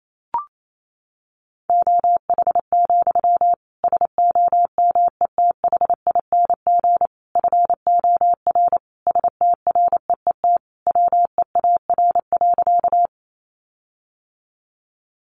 Morse code, heard as a keyed tone, about 28 wpm, 700 Hz. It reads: OH, SOMETHING FOR STREET WEAR.
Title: Sentences from Top 1500 Words 28wpm